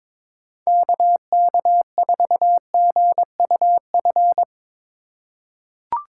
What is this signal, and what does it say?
Morse code, heard as a keyed tone, about 22 wpm, 700 Hz. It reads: KK4GUF
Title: USA Call Signs - 3x Character Spacing - 22wpm